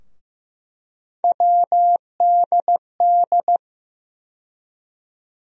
Morse code, heard as a keyed tone, about 16 wpm, 700 Hz. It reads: WDD
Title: USA Call Sign Postfixes - 3 Letters 15wpm